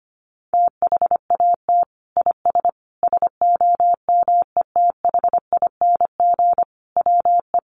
Morse code, heard as a keyed tone, about 25 wpm, 700 Hz. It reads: THAT IS SOMETHING WE
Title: Sentences from Top 1100 Words 25wpm